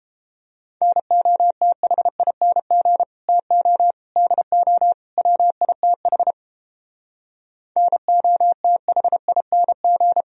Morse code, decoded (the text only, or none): NOTHING TO DO WITH NOTHING